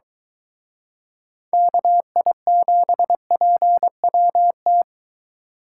Morse code, heard as a keyed tone, about 22 wpm, 700 Hz. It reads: KI7PWT